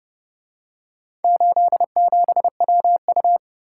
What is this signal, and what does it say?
Morse code, heard as a keyed tone, about 28 wpm, 700 Hz. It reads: 87WU